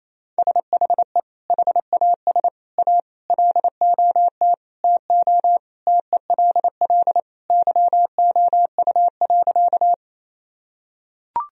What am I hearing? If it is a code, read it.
Morse code, heard as a keyed tone, about 27 wpm, 700 Hz. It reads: SHE HAS A LOT TO TELL YOU.
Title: Sentences from Top 1000 Words - Review Part 3 - 28wpm